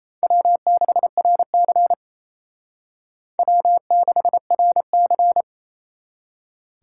Morse code, heard as a keyed tone, about 29 wpm, 700 Hz. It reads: W6RC W6RC
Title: Speed Racing - USA 1x2 Call Signs 28wpm